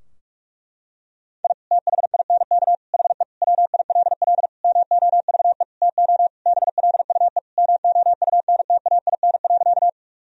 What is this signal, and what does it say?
Morse code, heard as a keyed tone, about 46 wpm, 700 Hz. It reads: I THINK HE WILL MOVE TO BLUE MOUNTAIN.
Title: Sentences from Top 1000 Words - Review Part 3 - 45wpm